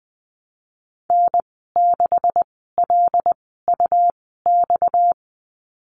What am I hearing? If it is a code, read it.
Morse code, heard as a keyed tone, about 20 wpm, 700 Hz. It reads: N 6 L U X
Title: 238 Lesson 38g - Letter q - Call Signs 20wpm